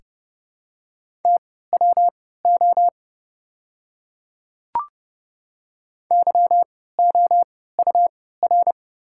Morse code, heard as a keyed tone, about 31 wpm, spaced as for 17 wpm, 690 Hz. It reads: TWO YOUR
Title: 3x Character Spacing - Top 200 Words - 30wpm